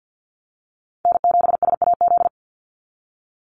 Morse code, heard as a keyed tone, about 50 wpm, 700 Hz. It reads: N7SAZ
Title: USA Call Signs 50wpm